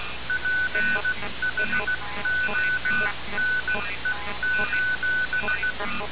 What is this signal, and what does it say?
Morse code, heard as a keyed tone, about 16 wpm, 1500 Hz, with noise. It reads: PHONE:
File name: radiocap / REC z